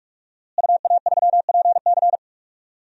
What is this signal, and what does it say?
Morse code, heard as a keyed tone, about 44 wpm, 700 Hz. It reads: UA3PC
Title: DXCC-100 Call Signs - 5 Characters - Repeated 3x - 45wpm